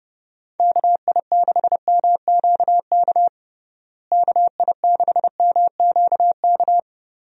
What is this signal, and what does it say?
Morse code, heard as a keyed tone, about 30 wpm, 700 Hz. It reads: KI6MQK KI6MQK
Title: USA Call Signs - 6 Characters - Repeated 3x - 30wpm